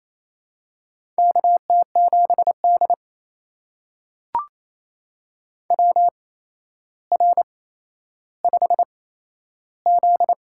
Morse code, heard as a keyed tone, about 28 wpm, spaced as for 12 wpm, 700 Hz. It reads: KT7D WR5Z
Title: USA Call Signs - 4 Characters - 8x Character Spacing - 28wpm